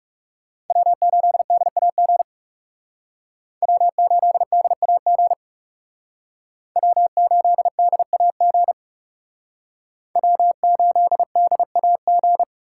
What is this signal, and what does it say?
Morse code, heard as a keyed tone, about 36 wpm, 700 Hz. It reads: W8DAG W8DAG W8DAG W8DAG
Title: Speed Racing - USA 1x3 Call Signs 30wpm